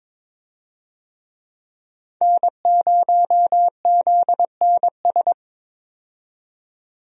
Morse code, heard as a keyed tone, about 21 wpm, 700 Hz. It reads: N0ZNS
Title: USA Call Signs - 5 Characters - 22wpm